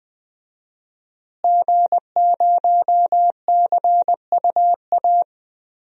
Morse code, heard as a keyed tone, about 20 wpm, 700 Hz. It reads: G0CUA
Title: DXCC-100 Call Signs - 5 Characters - Repeated 3x - 20wpm